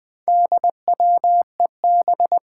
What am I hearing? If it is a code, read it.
Morse code, heard as a keyed tone, about 20 wpm, 700 Hz. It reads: DWEB
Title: Four Challenging Letter-Number Combinations 20wpm